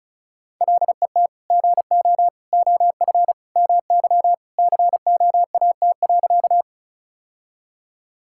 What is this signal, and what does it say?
Morse code, heard as a keyed tone, about 36 wpm, 700 Hz. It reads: LET GO OF MY COAT.
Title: Sentences from Top 1600 Words 35wpm